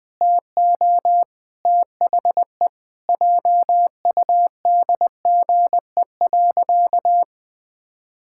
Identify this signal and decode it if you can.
Morse code, heard as a keyed tone, about 20 wpm, 700 Hz. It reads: TO THE JUDGE.
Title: Pangram Challenge v2 - 20wpm